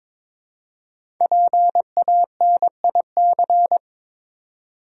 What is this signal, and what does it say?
Morse code, heard as a keyed tone, about 22 wpm, 700 Hz. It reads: PANIC
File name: 5 Letter Words - First 3 Letters Given - 22wpm